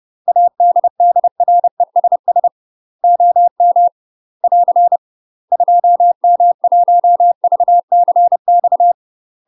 Morse code, heard as a keyed tone, about 30 wpm, 700 Hz. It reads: ADDRESS OM <AR> 2M1VCX
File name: UK MASTER QSOs TRAINING / 30WPM / UK MASTER QSOs 0038